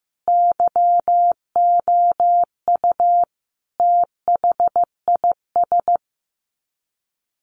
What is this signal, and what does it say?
Morse code, heard as a keyed tone, about 15 wpm, 700 Hz. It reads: YOU THIS
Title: Sets of 5 Words 15wpm